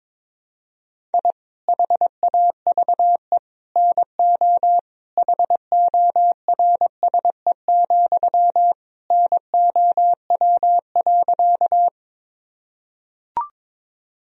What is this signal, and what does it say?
Morse code, heard as a keyed tone, about 22 wpm, 700 Hz. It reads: I HAVE NO HORSE, NOW.
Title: Sentences from Top 1000 Words - Review Part 1 - 22wpm